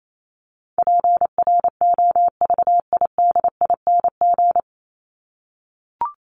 Morse code, heard as a keyed tone, about 28 wpm, 700 Hz. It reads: PROVIDING
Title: ING Suffix - Top 100 Words with Base Words 28wpm